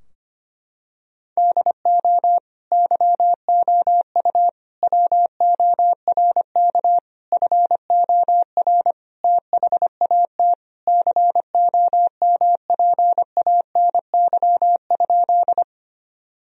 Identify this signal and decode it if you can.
Morse code, heard as a keyed tone, about 25 wpm, 700 Hz. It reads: DO YOU WORK FOR THAT COMPANY?
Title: Sentences from Top 1000 Words - Review Part 3 - 25wpm